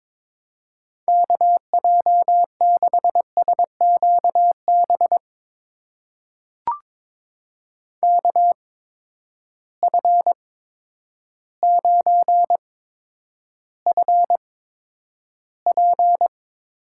Morse code, heard as a keyed tone, about 22 wpm, spaced as for 10 wpm, 700 Hz. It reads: KJ6SQB KF9FP
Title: USA Call Signs - 8x Character Spacing - 22wpm